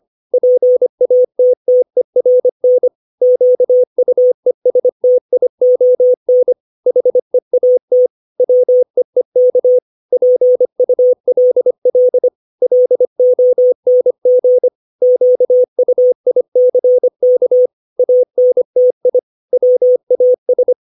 Morse code, heard as a keyed tone, about 25 wpm, 500 Hz. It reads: PATTERN QUESTION HEAT WEEK PULL LONG QUICK ANTI WAS
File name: 632 words / 632 Words 25WPM